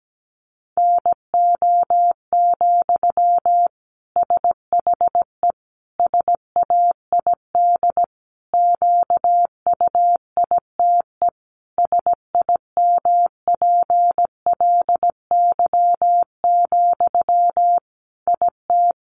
Morse code, heard as a keyed tone, about 17 wpm, 700 Hz. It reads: NO, SHE SAID QUITE SIMPLY, IT
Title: Sentences from Top 800 Words 17wpm